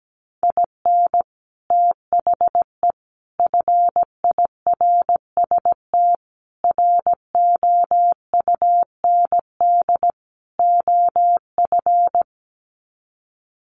Morse code, heard as a keyed tone, about 17 wpm, 700 Hz. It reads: IN THE FIRST ROUND OF
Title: Sets of 5 Words Encore 17wpm